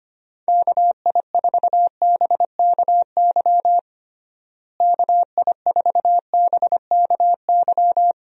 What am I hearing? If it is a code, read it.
Morse code, heard as a keyed tone, about 25 wpm, 700 Hz. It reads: KI4BKY KI4BKY
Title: USA Call Signs - 6 Characters - Repeated 2x - 25wpm